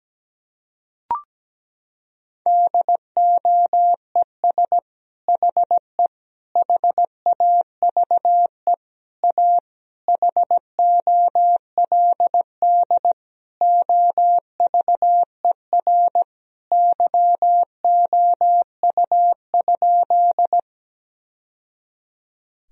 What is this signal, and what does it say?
Morse code, heard as a keyed tone, about 17 wpm, 700 Hz. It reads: DOES HE HAVE A HOLD OVER YOU?
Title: Sentences from Top 1000 Words - Review Part 2 - 17wpm